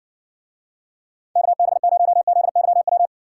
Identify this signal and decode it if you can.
Morse code, heard as a keyed tone, about 60 wpm, 700 Hz. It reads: KB0ZYW